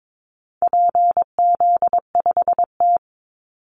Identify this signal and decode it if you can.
Morse code, heard as a keyed tone, about 22 wpm, 700 Hz. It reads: PZ5T
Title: Call Signs Repeated 2x - 22wpm